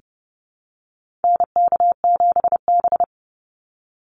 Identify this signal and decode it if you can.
Morse code, heard as a keyed tone, about 30 wpm, 700 Hz. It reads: NK7B
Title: Call Signs Repeated 2x - 30wpm